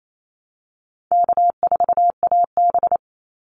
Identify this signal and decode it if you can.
Morse code, heard as a keyed tone, about 28 wpm, 700 Hz. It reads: K4AB
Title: Easy Call Signs Repeated 3x - 28wpm